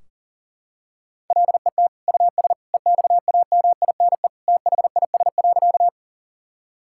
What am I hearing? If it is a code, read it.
Morse code, heard as a keyed tone, about 40 wpm, 700 Hz. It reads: LET US EXAMINE THIS.
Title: Sentences from Top 1600 Words 40wpm